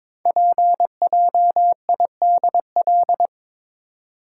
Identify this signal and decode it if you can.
Morse code, heard as a keyed tone, about 23 wpm, 700 Hz. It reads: PJIDL
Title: Code Groups - 5 Letters - 22wpm